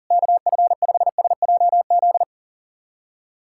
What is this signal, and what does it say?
Morse code, heard as a keyed tone, about 40 wpm, 700 Hz. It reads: KF5SJZ
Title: USA Callsign DE Callsign - 6 Characters - 40wpm